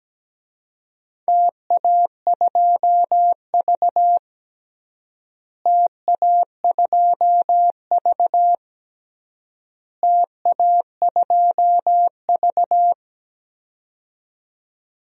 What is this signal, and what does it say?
Morse code, heard as a keyed tone, about 17 wpm, 700 Hz. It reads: TA2V TA2V TA2V
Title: DXCC-100 Call Signs - 4 Characters - Repeated 3x - 17wpm